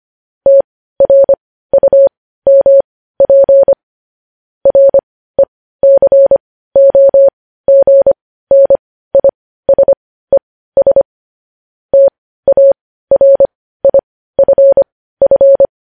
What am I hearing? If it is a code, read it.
Morse code, heard as a keyed tone, about 25 wpm, spaced as for 15 wpm, 550 Hz. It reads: TRUMP RECOGNISES TARIFF
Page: news-headlines-in-morse-code-at-15-wpm